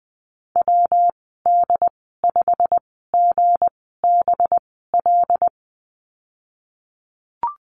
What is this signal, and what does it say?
Morse code, heard as a keyed tone, about 19 wpm, spaced as for 14 wpm, 700 Hz. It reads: WD5GBL
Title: 193 Lesson 31g - Letter v - Call Signs 20wpm